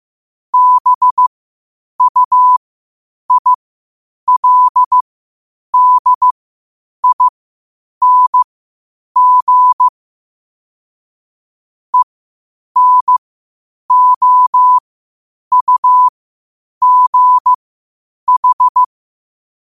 Morse code, heard as a keyed tone, about 15 wpm, spaced as for 8 wpm, 1000 Hz. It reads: BUILDING ENOUGH
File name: qotdfiles / 8wpm